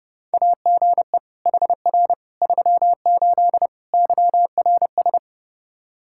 Morse code, heard as a keyed tone, about 30 wpm, 700 Hz. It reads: AGE HR 38 YRS
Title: K1USN SST QSO Practice 30wpm